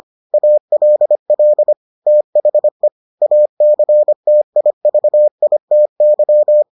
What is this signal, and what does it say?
Morse code, heard as a keyed tone, about 25 wpm, 600 Hz. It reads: ALL THE ACTIVITY